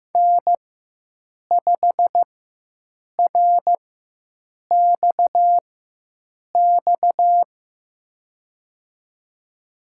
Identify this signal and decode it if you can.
Morse code, heard as a keyed tone, about 15 wpm, spaced as for 7 wpm, 700 Hz. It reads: N5RXX
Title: USA Call Signs - 5 Characters - 4x Character Spacing - 15wpm